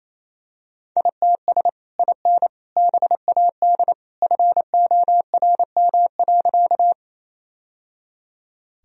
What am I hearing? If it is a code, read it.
Morse code, heard as a keyed tone, about 28 wpm, 700 Hz. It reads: ITS IN BAD FORM.
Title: Sentences from Top 1000 Words - Review Part 1 - 28wpm